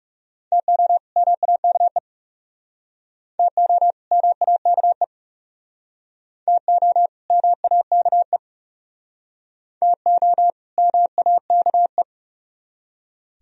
Morse code, heard as a keyed tone, about 36 wpm, 700 Hz. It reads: TO MAKE TO MAKE TO MAKE TO MAKE
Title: Speed-Racing - Sets of 2 Words 30wpm